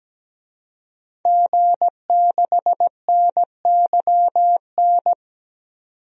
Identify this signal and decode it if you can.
Morse code, heard as a keyed tone, about 17 wpm, 700 Hz. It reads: G6NYN
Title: DXCC-100 Call Signs - 5 Characters - 17wpm